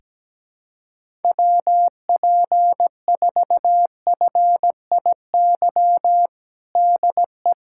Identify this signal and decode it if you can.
Morse code, heard as a keyed tone, about 17 wpm, 700 Hz. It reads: WP4FIY DE
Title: USA Callsign DE Callsign 17wpm